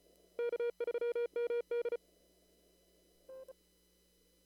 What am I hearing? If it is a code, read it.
Morse code, heard as a keyed tone, about 34 wpm, 3300 Hz. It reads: K3MD
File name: CW / kt5j wpx cw-op-k5pi